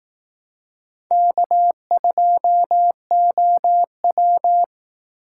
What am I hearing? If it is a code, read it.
Morse code, heard as a keyed tone, about 18 wpm, 700 Hz. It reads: K2OW
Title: Speed Racing - USA 1x2 Call Signs 15wpm